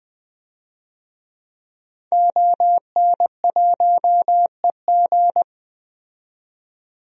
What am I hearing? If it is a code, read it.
Morse code, heard as a keyed tone, about 20 wpm, 700 Hz. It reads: ON1EG
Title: DXCC-100 Call Signs - 5 Characters - 20wpm